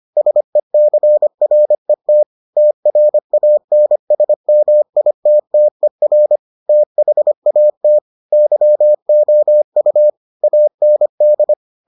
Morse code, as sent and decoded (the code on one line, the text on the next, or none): ... -.-. .-. - / - .-. .- -. ... -- .. - - . .-. / - .... .- - / -.-- --- ..- / .- -. -..
SECRET TRANSMITTER THAT YOU AND